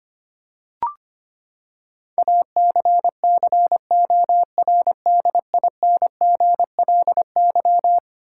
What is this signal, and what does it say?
Morse code, heard as a keyed tone, about 25 wpm, 700 Hz. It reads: ACCORDINGLY
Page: morse-code-ninja-practice-25wpm